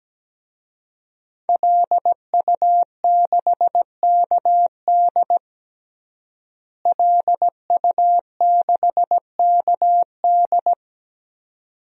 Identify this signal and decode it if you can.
Morse code, heard as a keyed tone, about 17 wpm, 700 Hz. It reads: LU6KD LU6KD
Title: DXCC-100 Call Signs - 5 Characters - Repeated 2x - 17wpm